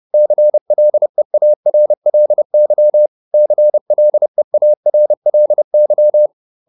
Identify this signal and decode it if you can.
Morse code, heard as a keyed tone, about 30 wpm, 600 Hz. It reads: CLEARLY CLEARLY